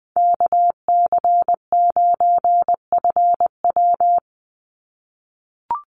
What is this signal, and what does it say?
Morse code, heard as a keyed tone, about 20 wpm, 700 Hz. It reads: KC9FW
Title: USA Call Signs 20wpm